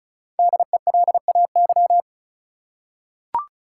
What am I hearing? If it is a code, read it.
Morse code, heard as a keyed tone, about 34 wpm, 700 Hz. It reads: DELAY
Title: Five Letter Words - 35wpm